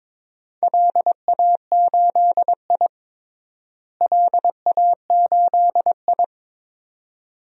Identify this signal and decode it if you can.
Morse code, heard as a keyed tone, about 22 wpm, 700 Hz. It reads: LA8I LA8I
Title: DXCC-100 Call Signs - 4 Characters - Repeated 2x - 22wpm